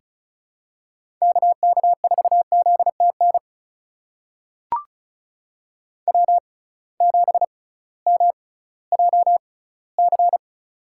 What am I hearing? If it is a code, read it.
Morse code, heard as a keyed tone, about 34 wpm, spaced as for 19 wpm, 700 Hz. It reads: KK4ZTN W7MJC
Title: USA Call Signs - 6x Character Spacing - 35wpm